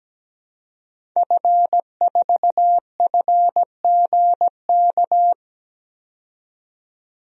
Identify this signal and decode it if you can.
Morse code, heard as a keyed tone, about 17 wpm, 700 Hz. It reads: F4FGK
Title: DXCC-100 Call Signs - 5 Characters - 17wpm